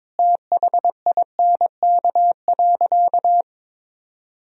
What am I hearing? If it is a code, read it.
Morse code, heard as a keyed tone, about 22 wpm, 700 Hz. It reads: THINK.
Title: Sentences from Top 1000 Words - Review Part 3 - 22wpm